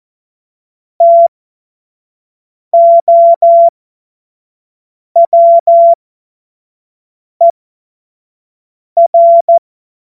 Morse code, heard as a keyed tone, about 14 wpm, spaced as for 5 wpm, 680 Hz. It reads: TOWER
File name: Morsekurs / Witze / Witz 30. Mai 2023 25 F70